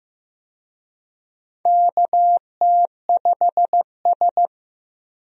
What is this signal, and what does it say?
Morse code, heard as a keyed tone, about 14 wpm, 700 Hz. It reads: KT5S
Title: USA Call Signs - 4 Characters - Repeated 3x - 15wpm